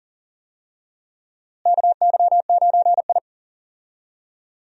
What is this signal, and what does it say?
Morse code, heard as a keyed tone, about 37 wpm, 700 Hz. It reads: KY9I
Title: USA Call Signs - 4 Characters - Repeated 3x - 40wpm